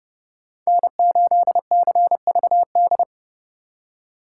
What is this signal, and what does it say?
Morse code, heard as a keyed tone, about 32 wpm, 700 Hz. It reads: N8CVD
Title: Speed Racing - USA 1x3 Call Signs 30wpm